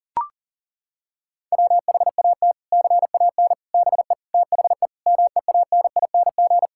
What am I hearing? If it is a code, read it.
Morse code, heard as a keyed tone, about 37 wpm, 700 Hz. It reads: WHAT CAN BE THE MEANING
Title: Sentences from Top 1000 Words - Review Part 6 - 40wpm